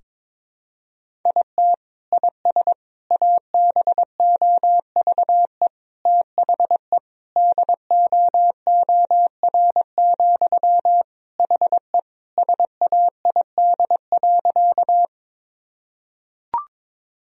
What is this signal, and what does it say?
Morse code, heard as a keyed tone, about 22 wpm, 700 Hz. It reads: IT IS ABOVE THE DOOR, HE SAID.
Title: Sentences from Top 1000 Words - Review Part 2 - 22wpm